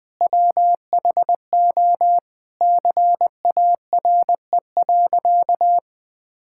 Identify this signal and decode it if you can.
Morse code, heard as a keyed tone, about 21 wpm, 700 Hz. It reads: WHO CARE.
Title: Sentences with 9-Word Phrases - 20wpm